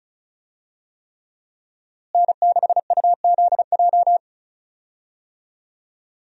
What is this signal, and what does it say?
Morse code, heard as a keyed tone, about 34 wpm, 700 Hz. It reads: N6UZJ
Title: USA Call Signs - 5 Characters - Repeated 3x - 35wpm